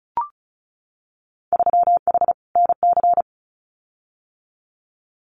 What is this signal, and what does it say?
Morse code, heard as a keyed tone, about 34 wpm, 700 Hz. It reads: 3H NC
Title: Winter Field Day Exchanges 35wpm